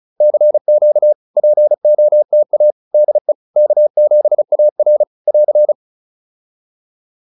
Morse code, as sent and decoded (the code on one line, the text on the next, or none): -.-. --.- / .--. --- - .- / -.. / -.- --... .- .-. / .-.-.
CQ POTA DE K7AR <AR>